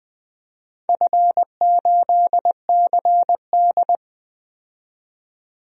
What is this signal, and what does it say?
Morse code, heard as a keyed tone, about 20 wpm, 700 Hz. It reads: F8CD
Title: DXCC-100 Call Signs - 4 Characters - 20wpm